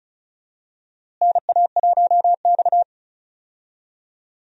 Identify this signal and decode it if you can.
Morse code, heard as a keyed tone, about 36 wpm, 700 Hz. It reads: NA1X
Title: USA Call Signs - 4 Characters - Repeated 2x - 35wpm